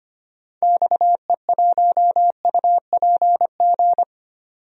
Morse code, heard as a keyed tone, about 25 wpm, 700 Hz. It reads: XE1UPG
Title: DXCC-100 Call Signs - 6 Characters - 25wpm